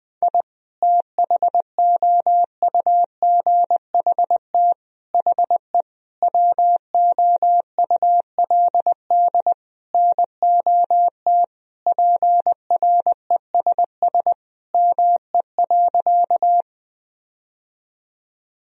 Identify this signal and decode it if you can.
Morse code, heard as a keyed tone, about 20 wpm, 700 Hz. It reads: I THOUGHT HE WOULD NOT PRESS ME.
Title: Sentences from Top 1000 Words - Review Part 1 - 20wpm